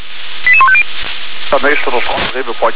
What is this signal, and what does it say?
Five-tone selective call (ZVEI), heard as quick runs of five tones, tones of about 70 ms, with speech. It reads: 88150